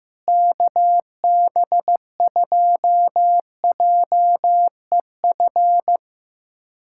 Morse code, heard as a keyed tone, about 15 wpm, 700 Hz. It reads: KB2JEF
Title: USA Callsign DE Callsign - 6 Characters - 15wpm